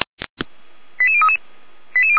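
Five-tone selective call (ZVEI), heard as quick runs of five tones, tones of about 70 ms. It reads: 88033
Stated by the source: ZVEI five-tone selective call received on an SDR.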